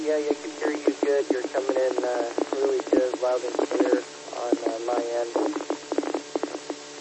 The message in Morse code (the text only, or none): none